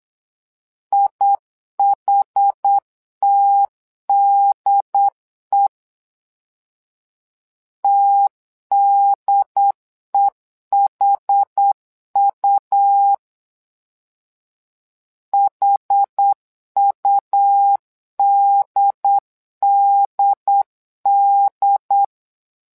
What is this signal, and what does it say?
Morse code, heard as a keyed tone, about 8 wpm, 800 Hz. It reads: IHTDE TDEHU HUDDD
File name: LEZIONI / 2 UHDN / 2 UHD 8wpm 2258